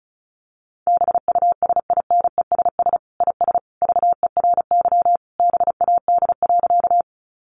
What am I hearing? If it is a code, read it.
Morse code, heard as a keyed tone, about 34 wpm, 700 Hz. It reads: BUSINESS IS VERY BAD.
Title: Sentences from Top 700 Words 35wpm